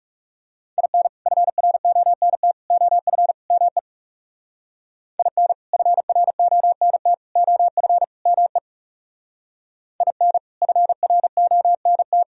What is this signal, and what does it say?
Morse code, heard as a keyed tone, about 35 wpm, 700 Hz. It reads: IN FRONT OF ME IN FRONT OF ME IN FRONT
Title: Speed-Racing - Sets of 4 Words 30wpm